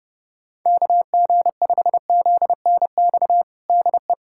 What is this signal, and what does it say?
Morse code, heard as a keyed tone, about 30 wpm, 700 Hz. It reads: KG5ZNX DE
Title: USA Callsign DE Callsign 30wpm